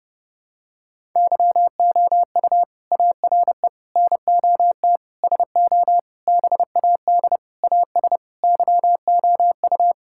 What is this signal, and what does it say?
Morse code, heard as a keyed tone, about 30 wpm, 700 Hz. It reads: YOU ARE NOT SO BAD AS YOU
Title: Sentences from Top 1000 Words - Review Part 6 - 30wpm